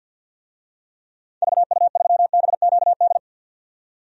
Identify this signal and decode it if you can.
Morse code, heard as a keyed tone, about 52 wpm, 700 Hz. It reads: VU3BQD
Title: DXCC-100 Call Signs - 6 Characters - 50wpm